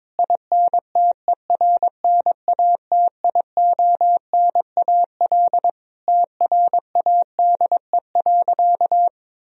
Morse code, heard as a keyed tone, about 22 wpm, 700 Hz. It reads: INTERNATIONAL TRADE.
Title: Sentences from Top 1000 Words - Review Part 4 - 22wpm